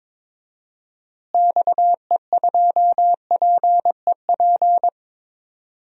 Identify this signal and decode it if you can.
Morse code, heard as a keyed tone, about 22 wpm, 700 Hz. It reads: XE2PEP
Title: DXCC-100 Call Signs - 6 Characters - Repeated 2x - 22wpm